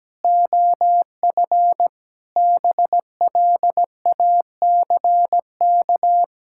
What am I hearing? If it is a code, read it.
Morse code, heard as a keyed tone, about 17 wpm, 700 Hz. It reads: OF BLACK
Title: Pangram Challenge v2 - 17wpm